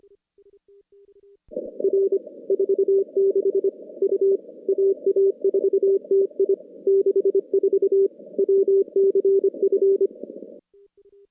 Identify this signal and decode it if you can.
Morse code, heard as a keyed tone, about 25 wpm, 410 Hz. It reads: R 46 U AA4TI 64 WCF
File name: contests / CW-SS-k5tr- -2300